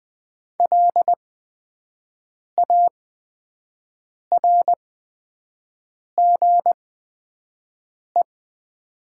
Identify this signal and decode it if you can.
Morse code, heard as a keyed tone, about 20 wpm, spaced as for 5 wpm, 700 Hz. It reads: LARGE
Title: 8x Character Spacing - Top 200 Words - 20wpm